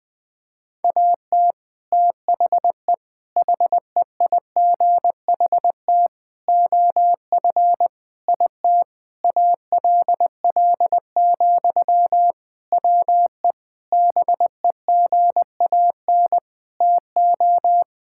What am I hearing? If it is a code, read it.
Morse code, heard as a keyed tone, about 20 wpm, 700 Hz. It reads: AT THE HEIGHT OF IT ALL, WE BEGAN TO
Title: Sentences from Top 1600 Words 20wpm